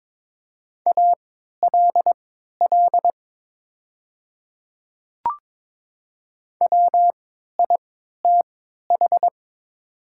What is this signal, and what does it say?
Morse code, heard as a keyed tone, about 22 wpm, spaced as for 13 wpm, 700 Hz. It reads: ALL WITH
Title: 3x Character Spacing - Top 100 Words - 22wpm